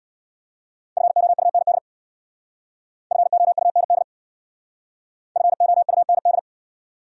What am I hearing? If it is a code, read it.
Morse code, heard as a keyed tone, about 62 wpm, 700 Hz. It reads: FOUND FOUND FOUND
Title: Speed-Racing Top 300 Words 50wpm